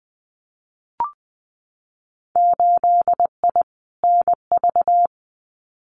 Morse code, heard as a keyed tone, about 20 wpm, 700 Hz. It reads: 8I NV
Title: Winter Field Day Exchanges Encore 20wpm